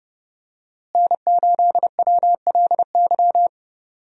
Morse code, heard as a keyed tone, about 30 wpm, 700 Hz. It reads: N8WLY
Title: USA Call Signs - 5 Characters - Repeated 3x - 30wpm